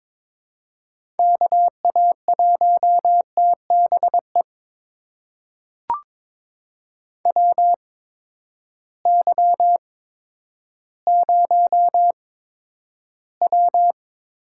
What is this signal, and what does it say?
Morse code, heard as a keyed tone, about 23 wpm, spaced as for 11 wpm, 700 Hz. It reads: KA1TBE WY0W
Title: USA Call Signs - 6 Characters - 8x Character Spacing - 22wpm